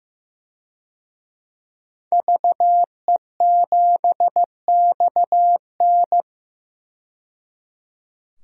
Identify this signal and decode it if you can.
Morse code, heard as a keyed tone, about 16 wpm, 700 Hz. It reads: VE7XN
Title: DXCC-100 Call Signs - 5 Characters - 15wpm